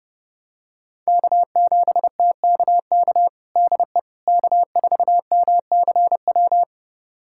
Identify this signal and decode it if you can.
Morse code, heard as a keyed tone, about 30 wpm, 700 Hz. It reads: K7TKK DE K4MCW
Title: USA Callsign DE Callsign - 5 Characters - 30wpm